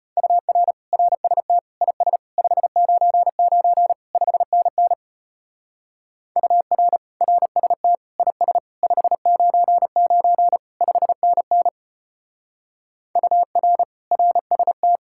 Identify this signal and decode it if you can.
Morse code, heard as a keyed tone, about 32 wpm, 700 Hz. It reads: UR RST IS 599 5NN UR RST IS 599 5NN UR RST